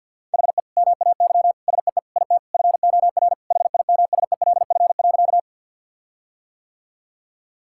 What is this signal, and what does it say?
Morse code, heard as a keyed tone, about 50 wpm, 700 Hz. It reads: HE MAY SEE IT FOR HIMSELF.
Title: Sentences from Top 1600 Words 50wpm